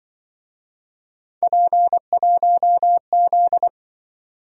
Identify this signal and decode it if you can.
Morse code, heard as a keyed tone, about 25 wpm, 700 Hz. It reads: P1Z
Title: Speed-Racing - Three Letter-Number Combinations - 20wpm